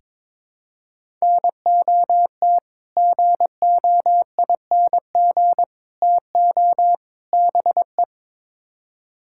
Morse code, heard as a keyed tone, about 22 wpm, 700 Hz. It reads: NOT GOING TO BE
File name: practice / 22wpm / Build A Phrase - 5-Word Phrases - 22wpm